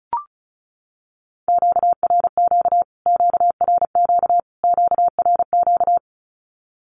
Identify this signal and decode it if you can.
Morse code, heard as a keyed tone, about 34 wpm, 700 Hz. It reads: QRQ QRQ QRQ
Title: Q-Codes Repeated 3x 35wpm